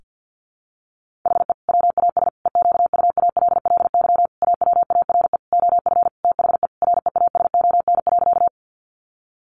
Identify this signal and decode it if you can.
Morse code, heard as a keyed tone, about 50 wpm, 700 Hz. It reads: HE WAS EQUALLY AWARE OF THE REASON.
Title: Sentences from Top 1400 Words 50wpm